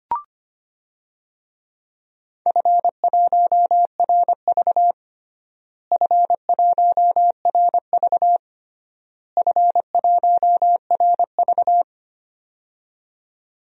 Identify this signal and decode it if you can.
Morse code, heard as a keyed tone, about 25 wpm, 700 Hz. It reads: F1RV F1RV F1RV
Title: DXCC-100 Call Signs - 4 Characters - Repeated 3x - 25wpm